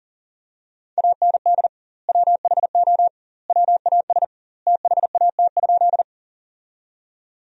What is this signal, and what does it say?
Morse code, heard as a keyed tone, about 37 wpm, 700 Hz. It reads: AND WHO WAS THAT?
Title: 2x Word-Spacing - Sentences from Top 100 Words - 40wpm